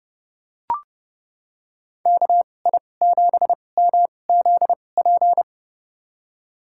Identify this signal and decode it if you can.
Morse code, heard as a keyed tone, about 30 wpm, spaced as for 22 wpm, 700 Hz. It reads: KI7MZP
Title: USA Call Signs - 6 Characters - 2x Character Spacing - 30wpm